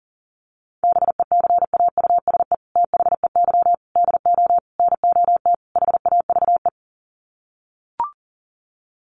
Morse code, heard as a keyed tone, about 40 wpm, 700 Hz. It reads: BECAUSE THEY DO NOT HAVE
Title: Sets of 5 Words Encore2 40wpm